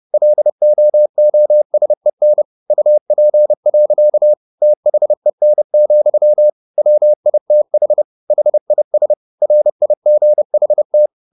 Morse code, heard as a keyed tone, about 30 wpm, 600 Hz. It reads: LOOSEN UP. THEN, WITH HIS RIGHT
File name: the old man and the sea / the old man and the sea 1 30wpm